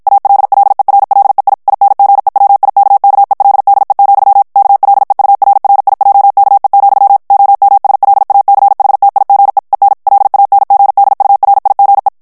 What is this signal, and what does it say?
Morse code, heard as a keyed tone, about 56 wpm, 800 Hz. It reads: ALLERLEI INGEWIKKELDE, KLEURRIJKE, OMSLACHTIGE EN LANGDURIGE
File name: HOLLANDAIS texte clair 2 vitesse 50 mots minute